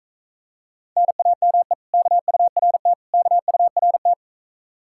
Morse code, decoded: NAME KURT KURT

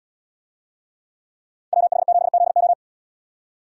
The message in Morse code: WH7ZO

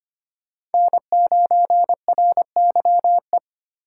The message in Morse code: N9RYE

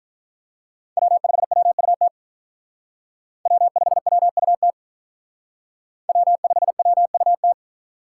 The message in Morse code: W5WUT W5WUT W5WUT

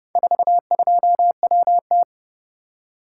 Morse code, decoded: 42WT

